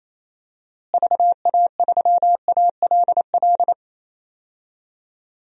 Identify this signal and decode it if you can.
Morse code, heard as a keyed tone, about 28 wpm, 700 Hz. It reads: VA3ALL
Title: DXCC-100 Call Signs - 6 Characters - Repeated 2x - 28wpm